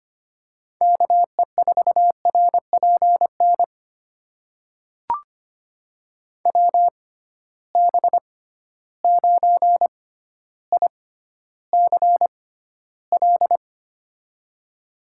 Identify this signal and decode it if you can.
Morse code, heard as a keyed tone, about 25 wpm, spaced as for 13 wpm, 700 Hz. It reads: KE4RPN WB9ICL
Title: USA Call Signs - 6 Characters - 6x Character Spacing - 25wpm